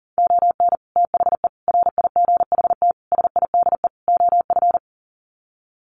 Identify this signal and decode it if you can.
Morse code, heard as a keyed tone, about 40 wpm, 700 Hz. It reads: ON THE RIGHT SIDE OF
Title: Sets of 5 Words Encore 40wpm